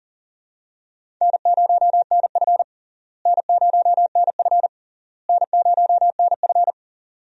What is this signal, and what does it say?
Morse code, heard as a keyed tone, about 40 wpm, 700 Hz. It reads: N0NF N0NF N0NF
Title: USA Call Signs - 4 Characters - Repeated 3x - 40wpm